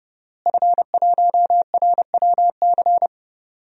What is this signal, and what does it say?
Morse code, heard as a keyed tone, about 30 wpm, 700 Hz. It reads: F1RWC
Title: DXCC-100 Call Signs - 5 Characters - Repeated 3x - 30wpm